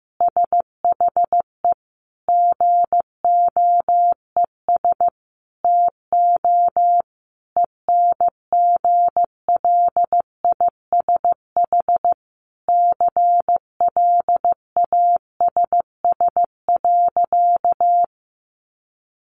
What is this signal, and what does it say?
Morse code, heard as a keyed tone, about 15 wpm, 700 Hz. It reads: SHE GOES TO ENGLISH CLASS.
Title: Additional Easy Phrases 15wpm